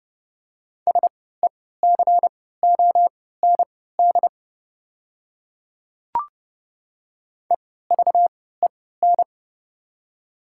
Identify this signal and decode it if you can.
Morse code, heard as a keyed tone, about 32 wpm, spaced as for 17 wpm, 700 Hz. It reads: SECOND EVEN
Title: 3x Character Spacing - Top 300 Words - 30wpm